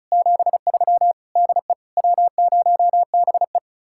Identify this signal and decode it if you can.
Morse code, heard as a keyed tone, about 43 wpm, 700 Hz. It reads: 73 DE W0BE
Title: Speed-Racing Common QSO Phrases - 35wpm